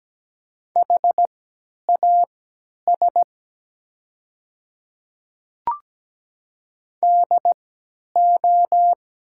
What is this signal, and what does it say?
Morse code, heard as a keyed tone, about 17 wpm, spaced as for 10 wpm, 700 Hz. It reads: HAS DO
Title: 3x Character Spacing - Top 200 Words - 17wpm